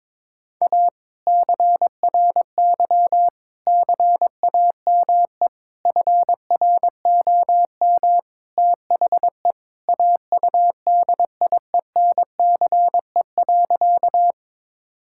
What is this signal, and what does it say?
Morse code, heard as a keyed tone, about 22 wpm, 700 Hz. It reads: A CRY CAME FROM THE AUDIENCE.
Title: Sentences from Top 1000 Words - Review Part 5 - 22wpm